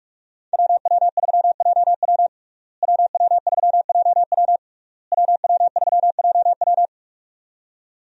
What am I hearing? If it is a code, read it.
Morse code, heard as a keyed tone, about 46 wpm, 700 Hz. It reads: WW3JW WW3JW WW3JW
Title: USA Call Signs - 5 Characters - Repeated 3x - 45wpm